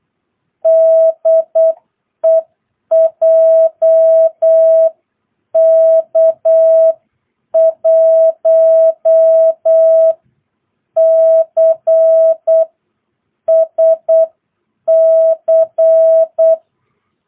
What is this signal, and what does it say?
Morse code, heard as a keyed tone, about 8 wpm, spaced as for 7 wpm, 650 Hz. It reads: DE JK1CSC
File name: jh8cbh / 2022cw